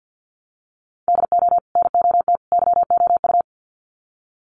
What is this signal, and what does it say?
Morse code, heard as a keyed tone, about 50 wpm, 700 Hz. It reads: DO NOT YOU
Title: Sets of 3 Words 50wpm